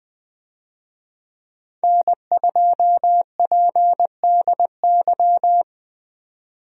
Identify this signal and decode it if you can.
Morse code, heard as a keyed tone, about 20 wpm, 700 Hz. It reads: N2PDY
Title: USA Call Signs - 5 Characters - 20wpm